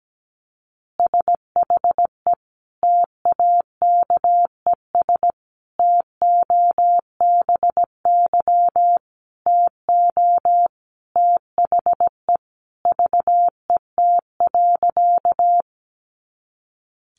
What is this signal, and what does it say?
Morse code, heard as a keyed tone, about 17 wpm, 700 Hz. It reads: SHE TAKES TOBY TO THE VET.
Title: Easy Phrases 17wpm